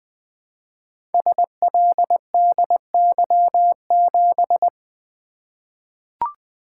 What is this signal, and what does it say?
Morse code, heard as a keyed tone, about 20 wpm, 700 Hz. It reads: SLDY7
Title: Code Groups - 5 Letter:Numbers - 20wpm